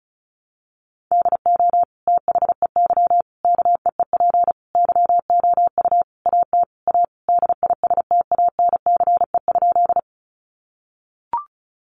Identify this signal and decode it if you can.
Morse code, heard as a keyed tone, about 34 wpm, 700 Hz. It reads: DO THEY KEEP YOU AT A DISTANCE?
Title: Sentences from Top 1200 Words 35wpm